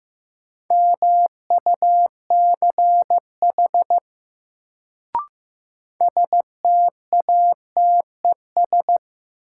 Morse code, heard as a keyed tone, about 15 wpm, 700 Hz. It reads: MUCH STATES
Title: Sets of 2 Words - New Word Drill - 15wpm